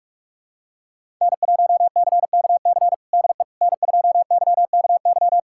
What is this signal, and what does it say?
Morse code, heard as a keyed tone, about 46 wpm, 700 Hz. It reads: N1CKC DE N2YKY
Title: USA Callsign DE Callsign - 5 Characters - 45wpm